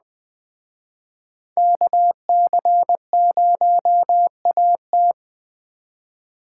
Morse code, heard as a keyed tone, about 21 wpm, 700 Hz. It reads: KC0AT